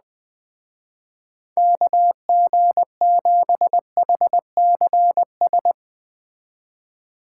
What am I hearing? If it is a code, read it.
Morse code, heard as a keyed tone, about 20 wpm, 700 Hz. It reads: KG7HCS
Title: Speed Racing - USA 2x3 Call Signs 17wpm